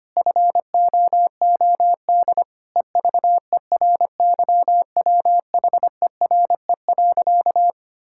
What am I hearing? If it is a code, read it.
Morse code, heard as a keyed tone, about 25 wpm, 700 Hz. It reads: FOOD EVERYWHERE.
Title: Sentences from Top 1600 Words 25wpm